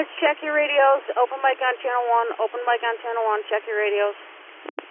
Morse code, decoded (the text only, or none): none